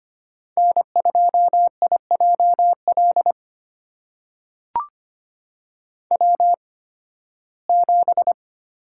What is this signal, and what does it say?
Morse code, heard as a keyed tone, about 25 wpm, spaced as for 16 wpm, 700 Hz. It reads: N2IJL W7
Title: USA Call Signs - 5 Characters - 8x Character Spacing - 25wpm